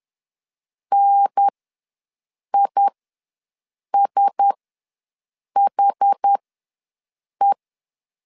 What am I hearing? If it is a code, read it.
Morse code, heard as a keyed tone, about 10 wpm, spaced as for 6 wpm, 800 Hz. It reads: NISHE
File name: LEZIONI / 6 PKX0 / 6 P ws 10wpm 3751